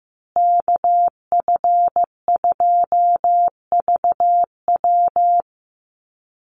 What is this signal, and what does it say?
Morse code, heard as a keyed tone, about 15 wpm, 700 Hz. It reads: KF2VW
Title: USA Call Signs Repeated 3x 15wpm